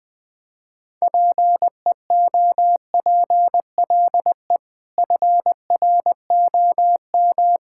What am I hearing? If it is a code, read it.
Morse code, heard as a keyed tone, about 20 wpm, 700 Hz. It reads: PEOPLE FROM
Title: Sets of 6 Words - 20wpm